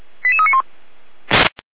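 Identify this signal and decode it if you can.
Five-tone selective call (ZVEI), heard as quick runs of five tones, tones of about 70 ms. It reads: 88391
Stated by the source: ZVEI five-tone selective call 88383, received on SDR.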